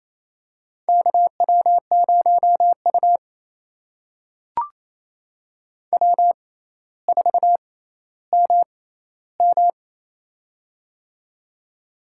Morse code, heard as a keyed tone, about 28 wpm, spaced as for 14 wpm, 700 Hz. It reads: KW0U W4MM
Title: USA Call Signs - 4 Characters - 6x Character Spacing - 28wpm